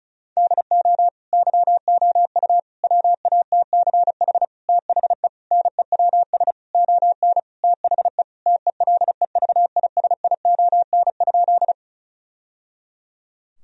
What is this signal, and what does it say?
Morse code, heard as a keyed tone, about 34 wpm, 700 Hz. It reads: DO YOU WATCH THE NEWS ON THE TELEVISION?
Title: Sentences from Top 1000 Words - Review Part 4 - 35wpm